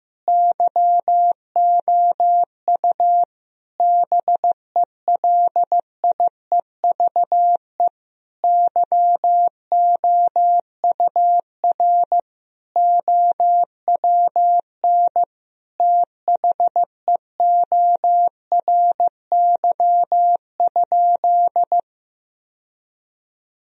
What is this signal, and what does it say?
Morse code, heard as a keyed tone, about 15 wpm, 700 Hz. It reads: YOU BELIEVE YOUR OWN THEORY?
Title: Sentences from Top 1000 Words - Review Part 3 - 15wpm